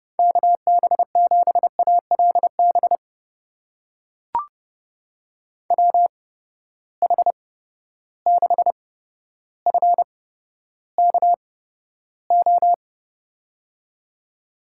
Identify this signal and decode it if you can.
Morse code, heard as a keyed tone, about 30 wpm, spaced as for 13 wpm, 700 Hz. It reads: KB7ALB WH6FKO
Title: USA Call Signs - 6 Characters - 8x Character Spacing - 30wpm